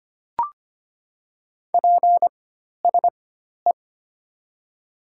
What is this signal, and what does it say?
Morse code, heard as a keyed tone, about 25 wpm, spaced as for 12 wpm, 700 Hz. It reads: PSE
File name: Top 90 QSO Elements - 4x Character Spacing - 25wpm